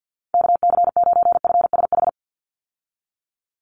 Morse code, heard as a keyed tone, about 50 wpm, 700 Hz. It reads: KC9FSH
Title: USA Call Signs 50wpm